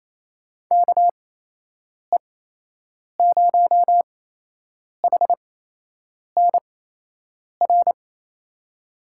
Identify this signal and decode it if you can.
Morse code, heard as a keyed tone, about 28 wpm, spaced as for 8 wpm, 700 Hz. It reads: KE0HNR
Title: USA Call Signs - 6 Characters - 8x Character Spacing - 28wpm